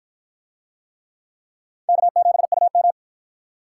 Morse code, heard as a keyed tone, about 49 wpm, 700 Hz. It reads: K7UM